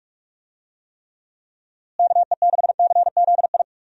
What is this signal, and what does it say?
Morse code, heard as a keyed tone, about 44 wpm, 700 Hz. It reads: KE6CZI
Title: USA Call Signs - 6 Characters - Repeated 2x - 45wpm